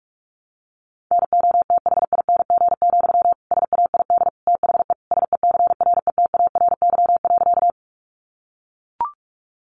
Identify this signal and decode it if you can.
Morse code, heard as a keyed tone, about 44 wpm, 700 Hz. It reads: NOTHING, SAID THE SECRETARY.
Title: Sentences from Top 1300 Words 45wpm